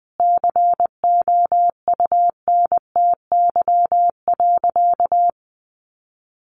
Morse code, heard as a keyed tone, about 20 wpm, 700 Hz. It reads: COUNTY.
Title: Sentences from Top 1200 Words 20wpm